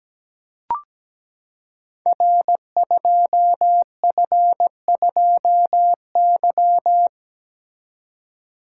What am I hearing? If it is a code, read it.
Morse code, heard as a keyed tone, about 17 wpm, 700 Hz. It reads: R2F2Y
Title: Code Groups - 5 Letter:Numbers - 17wpm